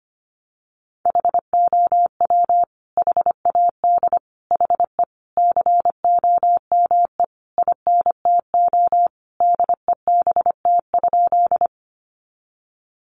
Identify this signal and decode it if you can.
Morse code, heard as a keyed tone, about 25 wpm, 700 Hz. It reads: HOW HAD HE COME INTO DEBT?
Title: Sentences from Top 1300 Words 25wpm